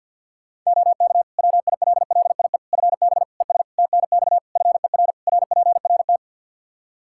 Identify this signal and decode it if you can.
Morse code, heard as a keyed tone, about 49 wpm, 700 Hz. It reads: OK WILLIE FB ES TNX FER RPRT
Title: Practice QSO - Exchange 2 - Info - 50wpm